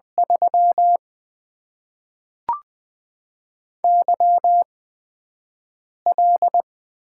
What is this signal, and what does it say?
Morse code, heard as a keyed tone, about 20 wpm, spaced as for 5 wpm, 700 Hz. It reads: 3 YL
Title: Top 90 QSO Elements - 8x Character Spacing - 20wpm